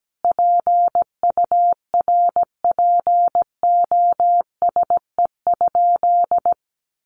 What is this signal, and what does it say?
Morse code, heard as a keyed tone, about 17 wpm, 700 Hz. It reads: PURPOSE?
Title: Sentences from Top 1500 Words 17wpm